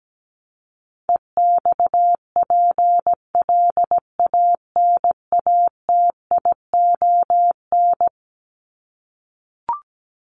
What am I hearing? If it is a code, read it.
Morse code, heard as a keyed tone, about 17 wpm, 700 Hz. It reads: EXPLANATION
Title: Top 1751-2000 Words 17wpm